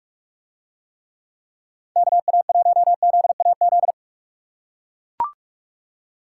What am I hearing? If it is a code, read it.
Morse code, heard as a keyed tone, about 43 wpm, 700 Hz. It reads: KA1ZAZ